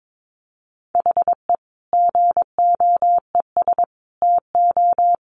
Morse code, heard as a keyed tone, about 22 wpm, 700 Hz. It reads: HE GOES TO
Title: Additional Easy Phrases 22wpm